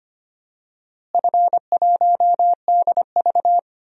Morse code, heard as a keyed tone, about 25 wpm, 700 Hz. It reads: F1DV
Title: DXCC-100 Call Signs - 4 Characters - Repeated 2x - 25wpm